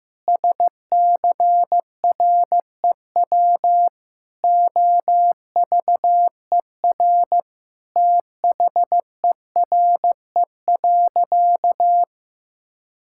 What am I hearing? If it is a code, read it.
Morse code, heard as a keyed tone, about 15 wpm, 700 Hz. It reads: SCREW OVER THERE.
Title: Sentences from Top 1600 Words 15wpm